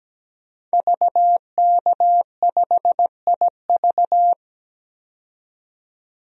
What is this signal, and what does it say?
Morse code, heard as a keyed tone, about 17 wpm, 700 Hz. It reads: VK5IV
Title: DXCC-100 Call Signs - 5 Characters - 17wpm